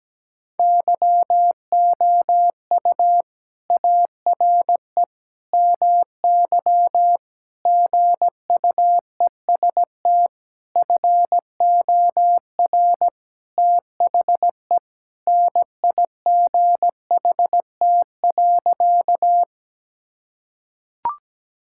Morse code, heard as a keyed tone, about 17 wpm, 700 Hz. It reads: YOU ARE MY GUEST FOR THE NIGHT.
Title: Sentences from Top 1600 Words 17wpm